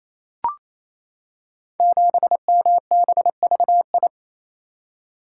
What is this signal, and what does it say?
Morse code, heard as a keyed tone, about 29 wpm, 700 Hz. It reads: 7MBVI
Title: Code Groups - 5 Letter:Numbers - 28wpm